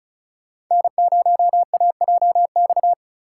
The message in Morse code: N0AJX

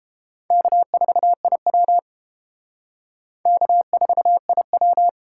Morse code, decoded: K4IW K4IW

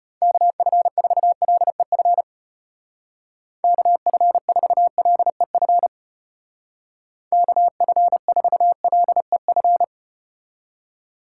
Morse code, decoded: KF4LEF KF4LEF KF4LEF

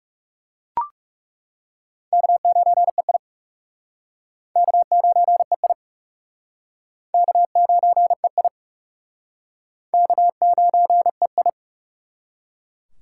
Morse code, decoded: K9EI K9EI K9EI K9EI

